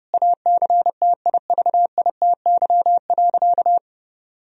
Morse code, heard as a keyed tone, about 30 wpm, 700 Hz. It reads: ACTIVITY.
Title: Sentences from Top 1000 Words - Review Part 4 - 30wpm